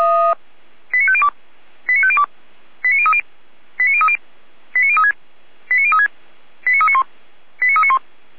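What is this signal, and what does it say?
Five-tone selective call (ZVEI), heard as quick runs of five tones, tones of about 70 ms. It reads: 89502 89502 89030 89030 89036 89036 89391 89391
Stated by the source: ZVEI five-tone alert received by SDR.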